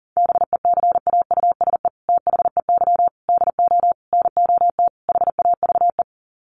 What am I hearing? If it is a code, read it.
Morse code, heard as a keyed tone, about 40 wpm, 700 Hz. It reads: BECAUSE THEY DO NOT HAVE
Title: Sets of 5 Words Encore2 40wpm